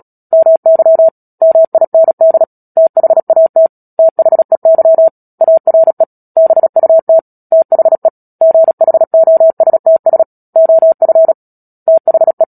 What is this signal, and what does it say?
Morse code, heard as a keyed tone, about 37 wpm, 650 Hz. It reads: MY MIND THAT THEY ARE BUT THE GHOSTS OF THE
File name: Book Two / 10-Chapter 10 - Epilogue - 36 WPM